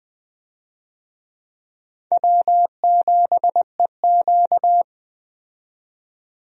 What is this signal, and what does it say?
Morse code, heard as a keyed tone, about 20 wpm, 700 Hz. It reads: W7EQ
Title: USA Call Signs - 4 Characters - 20wpm